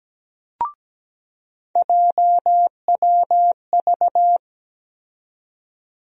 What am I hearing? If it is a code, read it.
Morse code, heard as a keyed tone, about 17 wpm, 700 Hz. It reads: JWV
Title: USA Call Sign Postfixes - 3 Letters 17wpm